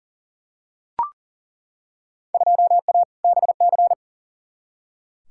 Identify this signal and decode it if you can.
Morse code, heard as a keyed tone, about 40 wpm, 700 Hz. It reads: 2A BC
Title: Field Day Exchanges Encore - 40wpm